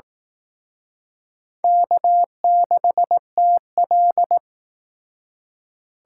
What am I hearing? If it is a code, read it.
Morse code, heard as a keyed tone, about 18 wpm, 700 Hz. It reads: K6TL
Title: Speed Racing - USA 1x2 Call Signs 15wpm